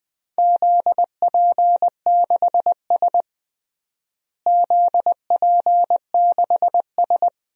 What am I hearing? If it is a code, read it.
Morse code, heard as a keyed tone, about 20 wpm, 700 Hz. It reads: ZP6S ZP6S
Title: DXCC-100 Call Signs - 4 Characters - Repeated 3x - 20wpm